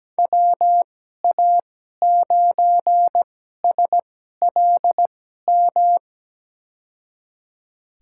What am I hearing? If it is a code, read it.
Morse code, heard as a keyed tone, about 17 wpm, spaced as for 12 wpm, 700 Hz. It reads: WA9SLM
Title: USA Call Signs - 2x Character Spacing - 17wpm